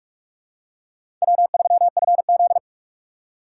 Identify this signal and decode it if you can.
Morse code, heard as a keyed tone, about 46 wpm, 700 Hz. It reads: W3FZ
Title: USA Call Signs - 6x Character Spacing - 45wpm